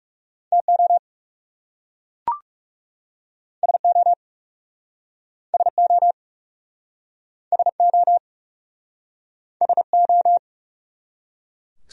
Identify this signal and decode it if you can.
Morse code, heard as a keyed tone, about 41 wpm, 700 Hz. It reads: TO SO SO SO SO